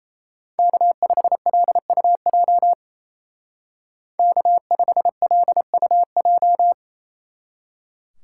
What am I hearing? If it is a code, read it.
Morse code, heard as a keyed tone, about 34 wpm, 700 Hz. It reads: K5LUJ K5LUJ
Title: Speed Racing - USA Call Signs 28wpm